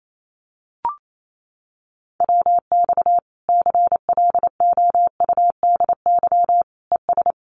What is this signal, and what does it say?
Morse code, heard as a keyed tone, about 28 wpm, 700 Hz. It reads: WX CLOUDY ES
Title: Common QSO Phrases 28wpm